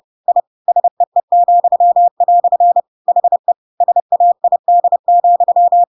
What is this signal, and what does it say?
Morse code, heard as a keyed tone, about 30 wpm, 700 Hz. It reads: I SEE," HE SAID,
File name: Pinocchio / Pinocchio ch1 30 30 ik1hge